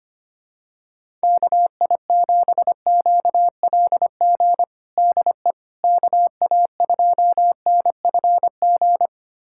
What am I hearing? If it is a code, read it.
Morse code, heard as a keyed tone, about 25 wpm, 700 Hz. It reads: KI7QLG DE KA2NFG
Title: USA Callsign DE Callsign - 6 Characters - 25wpm